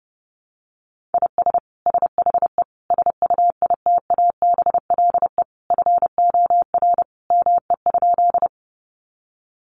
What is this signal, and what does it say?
Morse code, heard as a keyed tone, about 30 wpm, 700 Hz. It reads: IS SHE SUITABLE FOR ME?
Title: Sentences from Top 1400 Words 30wpm